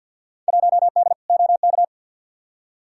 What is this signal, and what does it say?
Morse code, heard as a keyed tone, about 47 wpm, 700 Hz. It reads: 1D OK
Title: Field Day Exchanges - 50wpm